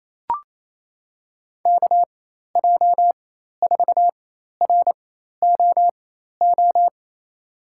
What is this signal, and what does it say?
Morse code, heard as a keyed tone, about 28 wpm, spaced as for 13 wpm, 700 Hz. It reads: KJ4ROO
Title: USA Call Signs - 6 Characters - 4x Character Spacing - 28wpm